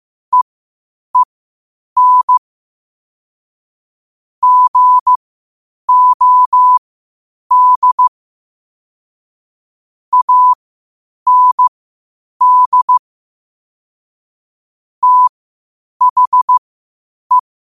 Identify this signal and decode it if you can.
Morse code, heard as a keyed tone, about 15 wpm, spaced as for 8 wpm, 1000 Hz. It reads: EEN GOD AND THE
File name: qotdfiles / 8wpm